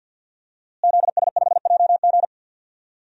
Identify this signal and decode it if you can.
Morse code, heard as a keyed tone, about 46 wpm, 700 Hz. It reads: ZS5JG
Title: DXCC-100 Call Signs - 5 Characters - Repeated 3x - 50wpm